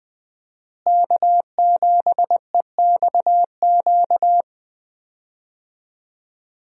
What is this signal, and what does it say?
Morse code, heard as a keyed tone, about 20 wpm, 700 Hz. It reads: K7EXQ
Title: USA Call Signs - 5 Characters - Repeated 3x - 20wpm